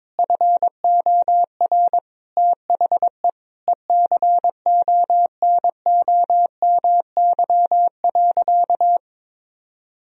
Morse code, heard as a keyed tone, about 22 wpm, 700 Hz. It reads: FOR THE ECONOMY.
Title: Sentences from Top 1000 Words - Review Part 4 - 22wpm